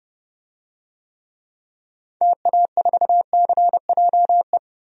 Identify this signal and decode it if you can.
Morse code, heard as a keyed tone, about 30 wpm, 700 Hz. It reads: TA4CJE
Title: DXCC-100 Call Signs - 6 Characters - 30wpm